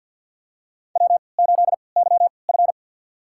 Morse code, heard as a keyed tone, about 47 wpm, spaced as for 28 wpm, 700 Hz. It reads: W8YF
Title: USA Call Signs - 4 Characters - 3x Character Spacing - 50wpm